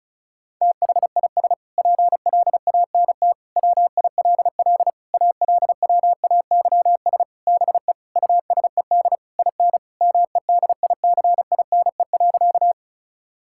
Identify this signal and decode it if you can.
Morse code, heard as a keyed tone, about 36 wpm, 700 Hz. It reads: THIS PLANT WILL ALWAYS BE USED IN MEDICINE.
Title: Sentences from Top 1000 Words - Review Part 6 - 35wpm